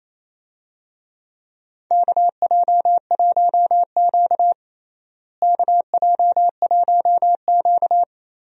Morse code, heard as a keyed tone, about 27 wpm, 700 Hz. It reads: KJ1Q KJ1Q
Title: USA Call Signs - 4 Characters - Repeated 2x - 28wpm